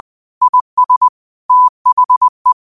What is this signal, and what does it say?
Morse code, heard as a keyed tone, about 20 wpm, 1000 Hz. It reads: IS THE
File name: qotdfiles / 20wpm